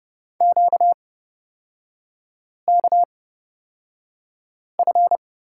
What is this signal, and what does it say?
Morse code, heard as a keyed tone, about 31 wpm, spaced as for 5 wpm, 690 Hz. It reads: QKF
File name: course / 30:10wpm / 236 Lesson 38e - Letter q - Three Characters 30wpm